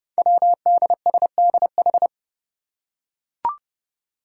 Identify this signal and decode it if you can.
Morse code, heard as a keyed tone, about 29 wpm, 700 Hz. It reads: WDSDH
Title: Five Challenging Letter-Number Combinations 30wpm